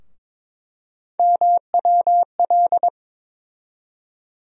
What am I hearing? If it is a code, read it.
Morse code, heard as a keyed tone, about 21 wpm, 700 Hz. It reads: MWL
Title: USA Call Sign Postfixes - 3 Letters 22wpm